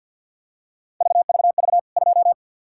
Morse code, heard as a keyed tone, about 51 wpm, 700 Hz. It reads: VVV 2
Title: K1USN SST QSO Practice 50wpm